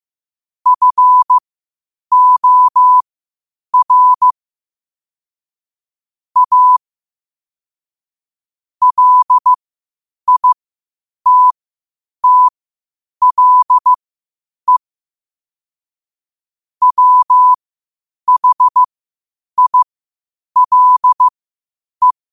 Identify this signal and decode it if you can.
Morse code, heard as a keyed tone, about 15 wpm, spaced as for 8 wpm, 1000 Hz. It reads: FOR A LITTLE WHILE
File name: qotdfiles / 8wpm